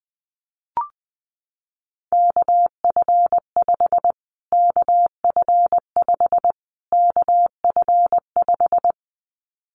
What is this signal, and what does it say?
Morse code, heard as a keyed tone, about 20 wpm, 700 Hz. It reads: KF5 KF5 KF5
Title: Callsign Prefixes for Top 25 DXCC - Repeated 3x - 20wpm